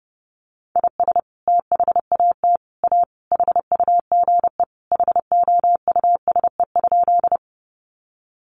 Morse code, heard as a keyed tone, about 30 wpm, 700 Hz. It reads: IS THAT A HUGE HOUSE?
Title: Sentences from Top 1100 Words 30wpm